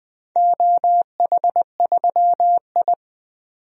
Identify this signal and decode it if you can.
Morse code, heard as a keyed tone, about 21 wpm, 700 Hz. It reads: OH3I
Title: DXCC-100 Call Signs - 4 Characters - 20wpm